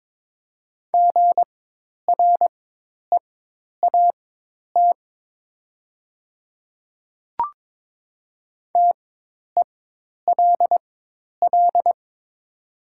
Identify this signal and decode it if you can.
Morse code, heard as a keyed tone, about 22 wpm, spaced as for 10 wpm, 700 Hz. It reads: GREAT TELL
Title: 4x Character Spacing - Top 200 Words - 22wpm